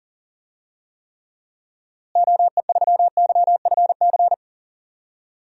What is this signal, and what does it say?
Morse code, heard as a keyed tone, about 40 wpm, 700 Hz. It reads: OE3YFC
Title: DXCC-100 Call Signs - 6 Characters - Repeated 2x - 40wpm